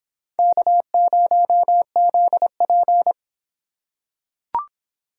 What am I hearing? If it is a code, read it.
Morse code, heard as a keyed tone, about 23 wpm, 700 Hz. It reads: K0ZP